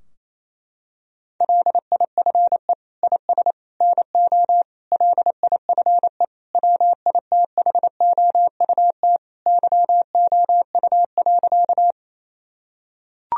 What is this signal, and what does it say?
Morse code, heard as a keyed tone, about 28 wpm, 700 Hz. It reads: LIFE IS NO LIFE WITHOUT YOU.
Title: Sentences from Top 1000 Words - Review Part 3 - 28wpm